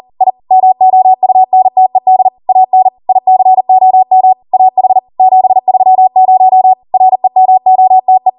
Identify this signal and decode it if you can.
Morse code, heard as a keyed tone, about 40 wpm, 750 Hz. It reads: I MOUNTED AN ICOM AH 730 REMOTE